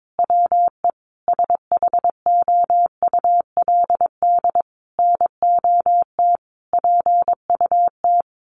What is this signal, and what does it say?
Morse code, heard as a keyed tone, about 22 wpm, 700 Hz. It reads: WE SHOULD NOT PUT
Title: Sentences from Top 1100 Words 22wpm